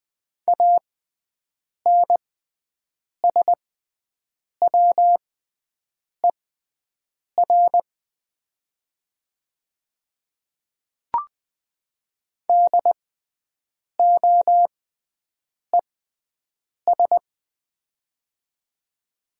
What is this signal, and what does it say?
Morse code, heard as a keyed tone, about 20 wpm, spaced as for 7 wpm, 700 Hz. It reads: ANSWER DOES
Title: 6x Character Spacing - Top 300 Words - 20wpm